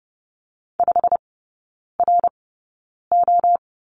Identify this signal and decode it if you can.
Morse code, heard as a keyed tone, about 30 wpm, spaced as for 9 wpm, 700 Hz. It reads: 5RO
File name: course / 30:10wpm / 087 Lesson 15e - Number 5 - Three Characters 30wpm